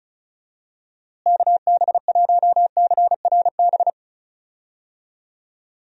Morse code, heard as a keyed tone, about 34 wpm, 700 Hz. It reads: KB1CRB
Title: USA Call Signs - 6 Characters - Repeated 2x - 35wpm